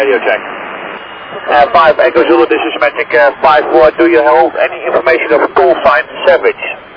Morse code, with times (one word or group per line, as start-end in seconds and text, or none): none